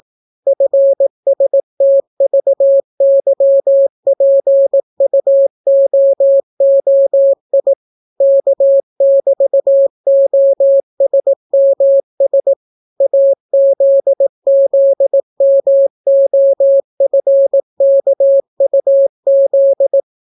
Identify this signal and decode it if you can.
Morse code, heard as a keyed tone, about 18 wpm, 550 Hz. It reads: FSTVYPUOOI K<BT>OSMS AZZMOFKUZ